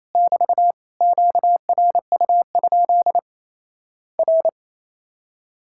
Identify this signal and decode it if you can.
Morse code, heard as a keyed tone, about 27 wpm, 700 Hz. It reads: <BT> QRU? R
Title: Worked All Europe DX Contest - QTC Exchange - 28wpm